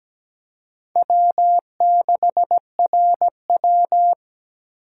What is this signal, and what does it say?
Morse code, heard as a keyed tone, about 17 wpm, 700 Hz. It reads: W6RW
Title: USA Call Signs - 4 Characters - 3x Character Spacing - 17wpm